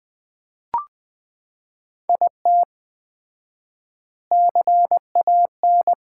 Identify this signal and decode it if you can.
Morse code, heard as a keyed tone, about 20 wpm, 700 Hz. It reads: IT CAN
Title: 4x Word-Spacing - Sets of 3 Words - 20wpm